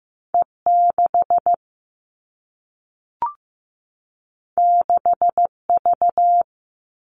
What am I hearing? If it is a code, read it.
Morse code, heard as a keyed tone, about 15 wpm, 700 Hz. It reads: E 6 6 V
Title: Two Challenging Letter-Number Combinations 15wpm